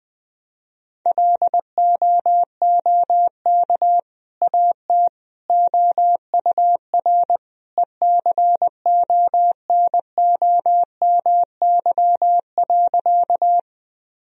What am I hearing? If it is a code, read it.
Morse code, heard as a keyed tone, about 20 wpm, 700 Hz. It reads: LOOK AT OUR ECONOMY.
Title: Sentences from Top 1000 Words - Review Part 3 - 20wpm